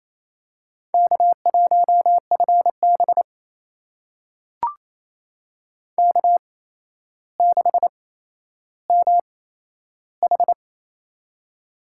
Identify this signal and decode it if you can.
Morse code, heard as a keyed tone, about 27 wpm, spaced as for 12 wpm, 700 Hz. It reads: K1FB K6MH
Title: USA Call Signs - 4 Characters - 8x Character Spacing - 28wpm